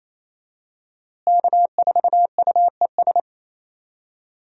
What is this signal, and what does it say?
Morse code, heard as a keyed tone, about 27 wpm, 700 Hz. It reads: K4UES
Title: USA Call Signs - 5 Characters - Repeated 3x - 28wpm